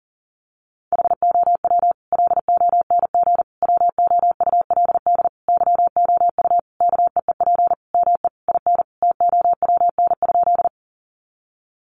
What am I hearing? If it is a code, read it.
Morse code, heard as a keyed tone, about 40 wpm, 700 Hz. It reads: HOW LONG WOULD YOU KEEP ME IN TOWN?
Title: Sentences from Top 500 Words 40wpm